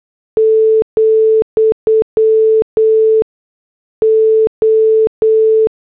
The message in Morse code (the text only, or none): , O